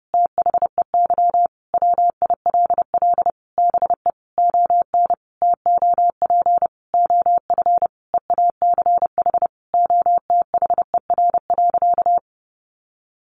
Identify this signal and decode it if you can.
Morse code, heard as a keyed tone, about 30 wpm, 700 Hz. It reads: THEY WILL BE ON TOP OF EACH OTHER.
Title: Sentences from Top 700 Words 30wpm